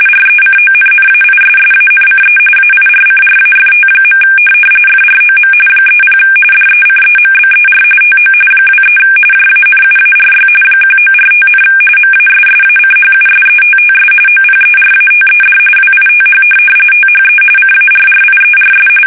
RTTY, about 45.45 baud. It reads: RECONNAISSANCE AIRCRAFT. FROM 30 DECEMBER TO 23 JANUARY 1945, MASSACHUSETTS OPERATED WITH THE CARRIERS
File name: fsk850 / USS-Massachusetts